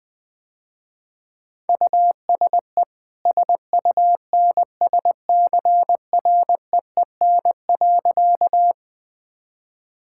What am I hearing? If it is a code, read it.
Morse code, heard as a keyed tone, about 20 wpm, 700 Hz. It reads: USE SUNSCREEN.
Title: Easy Sentences with Hints - 20wpm